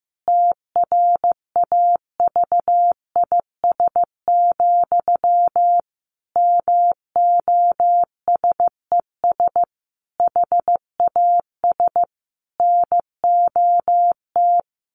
Morse code, heard as a keyed tone, about 15 wpm, 700 Hz. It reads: TRAVIS, MOSES HAS NOT
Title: A Sound Of Thunder 15wpm